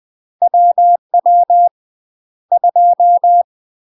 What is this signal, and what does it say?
Morse code, heard as a keyed tone, about 20 wpm, spaced as for 12 wpm, 700 Hz. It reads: WW2
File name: UK MASTER QSOs TRAINING / 20WPM / UK MASTER QSOs 0059